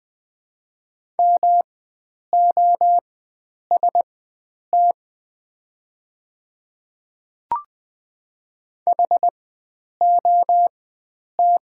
Morse code, heard as a keyed tone, about 20 wpm, spaced as for 9 wpm, 700 Hz. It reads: MOST HOT
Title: 4x Character Spacing - Top 100 Words - 20wpm